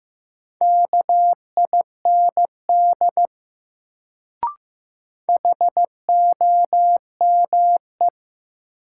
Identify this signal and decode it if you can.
Morse code, heard as a keyed tone, about 15 wpm, 700 Hz. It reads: KIND HOME
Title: Sentences from Top 200 Words - New Word Drill - 15wpm